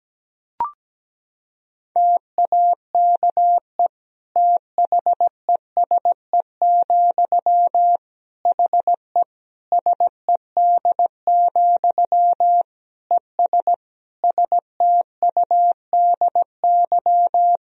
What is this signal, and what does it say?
Morse code, heard as a keyed tone, about 17 wpm, 700 Hz. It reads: TAKE THESE, HE SED, ES STUDY
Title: Sentences from Top 300 Words and Abbreviations 17wpm